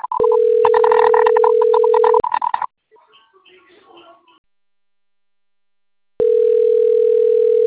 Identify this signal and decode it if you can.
Ringback tone (North America), 440 + 480 Hz, with speech.
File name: rg-1600-unknown- - - .84437